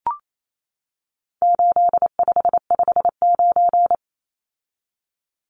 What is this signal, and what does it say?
Morse code, heard as a keyed tone, about 28 wpm, 700 Hz. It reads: 8559
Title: Numbers - 4-Digit 28wpm